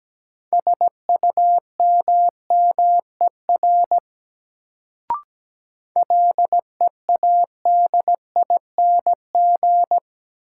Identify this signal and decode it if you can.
Morse code, heard as a keyed tone, about 17 wpm, 700 Hz. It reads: SUMMER LEADING
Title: Sentences from Top 1100 Words - New Word Drill Alt - 17wpm